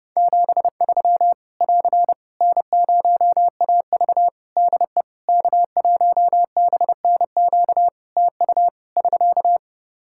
Morse code, heard as a keyed tone, about 30 wpm, 700 Hz. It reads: 73 <AR> N0AV DE K1BNQ TU <SK>
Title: Practice QSO - Exchange 4 - Ending - 30wpm